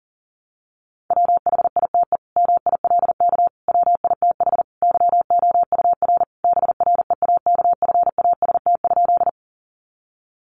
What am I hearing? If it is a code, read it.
Morse code, heard as a keyed tone, about 40 wpm, 700 Hz. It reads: WHITE MILK WITH YOUR BREAKFAST?
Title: Sentences from Top 1500 Words 40wpm